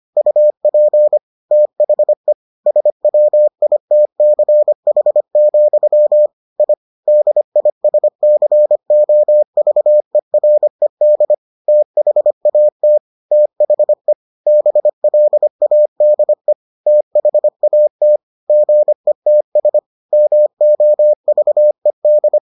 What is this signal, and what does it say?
Morse code, heard as a keyed tone, about 25 wpm, 600 Hz. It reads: UP THE SWITCH, I DISCOVERED THAT THE BLADE THAT GETS MOVED